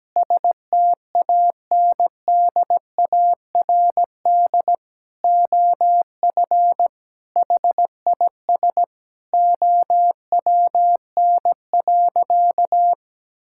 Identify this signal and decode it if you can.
Morse code, heard as a keyed tone, about 17 wpm, 700 Hz. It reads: STANDARD OF HIS OWN.
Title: Sentences from Top 1000 Words - Review Part 6 - 17wpm